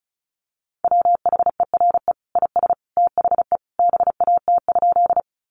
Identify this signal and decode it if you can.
Morse code, heard as a keyed tone, about 34 wpm, 700 Hz. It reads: WHERE IS THE BAT?
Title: Sentences from Top 1500 Words 35wpm